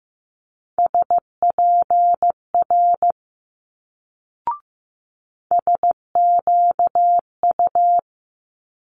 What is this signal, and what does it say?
Morse code, heard as a keyed tone, about 15 wpm, 700 Hz. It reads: SPR SQU
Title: Multi-letter Phonemes Quick Fire 15wpm